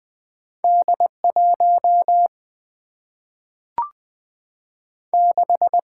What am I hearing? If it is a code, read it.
Morse code, heard as a keyed tone, about 20 wpm, 700 Hz. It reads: D1 6
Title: Single-Letter Single-Number 20wpm